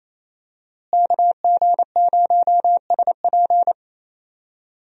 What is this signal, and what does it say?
Morse code, heard as a keyed tone, about 28 wpm, 700 Hz. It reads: KG0SP
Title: USA Call Signs - 5 Characters - Repeated 3x - 28wpm